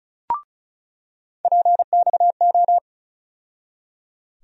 Morse code, heard as a keyed tone, about 36 wpm, 700 Hz. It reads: PXO
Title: USA Call Sign Postfixes - 3 Letters 35wpm